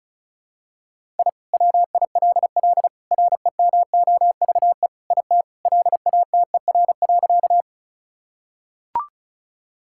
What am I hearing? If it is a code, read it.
Morse code, heard as a keyed tone, about 36 wpm, 700 Hz. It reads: I WILL REMOVE IT LATER.
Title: Sentences from Top 1000 Words - Review Part 5 - 35wpm